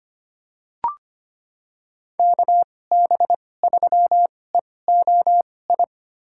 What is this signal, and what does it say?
Morse code, heard as a keyed tone, about 25 wpm, spaced as for 18 wpm, 700 Hz. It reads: KB3EOI
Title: USA Call Signs - 6 Characters - 2x Character Spacing - 25wpm